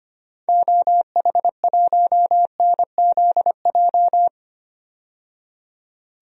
Morse code, heard as a keyed tone, about 25 wpm, 700 Hz. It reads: OH1NZJ
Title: DXCC-100 Call Signs - 6 Characters - 25wpm